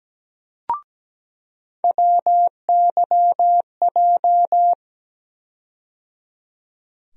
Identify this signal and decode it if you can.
Morse code, heard as a keyed tone, about 17 wpm, 700 Hz. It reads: WYJ
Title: USA Call Sign Postfixes - 3 Letters 17wpm